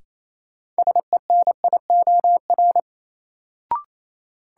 Morse code, heard as a keyed tone, about 28 wpm, 700 Hz. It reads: SENIOR
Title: Sets of 5 Words Encore - New Word Drill - 28wpm